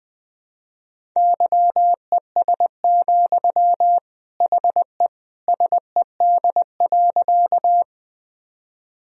Morse code, heard as a keyed tone, about 20 wpm, 700 Hz. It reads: YES, HE SED.
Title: Sentences from Top 500 Words and Abbreviations 20wpm